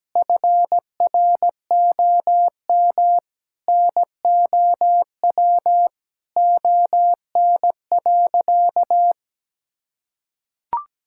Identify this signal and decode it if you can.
Morse code, heard as a keyed tone, about 17 wpm, 700 Hz. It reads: FROM NOW ON.
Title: Sentences from Top 1000 Words - Review Part 6 - 17wpm